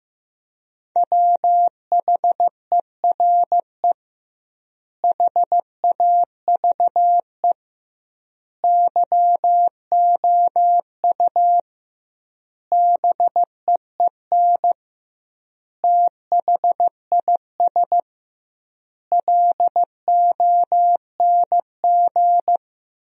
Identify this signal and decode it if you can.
Morse code, heard as a keyed tone, about 15 wpm, 700 Hz. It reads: WHERE HAVE YOU BEEN THIS LONG
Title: 2x Word-Spacing - Sentences from Top 300 Words - 15wpm